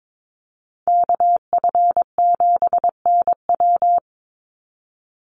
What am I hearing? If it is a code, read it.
Morse code, heard as a keyed tone, about 22 wpm, 700 Hz. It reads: KF7NW
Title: USA Call Signs 22wpm